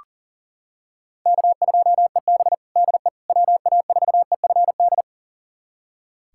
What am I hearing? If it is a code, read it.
Morse code, heard as a keyed tone, about 40 wpm, 700 Hz. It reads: K2EB DE WA4EFD
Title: USA Callsign DE Callsign 40wpm